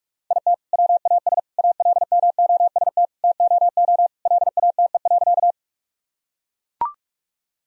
Morse code, heard as a keyed tone, about 42 wpm, 700 Hz. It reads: IT WAS ALMOST TOO LATE.
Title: Sentences from Top 1000 Words - Review Part 4 - 45wpm